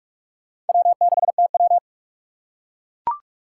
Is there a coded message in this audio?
Yes: Morse code from 0.7 to 1.8 s, heard as a keyed tone, about 42 wpm, 700 Hz. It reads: W6TW